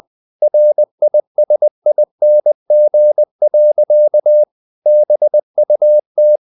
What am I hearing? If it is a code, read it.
Morse code, heard as a keyed tone, about 20 wpm, 600 Hz. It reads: RISING. BUT
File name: the old man and the sea part 2 20wpm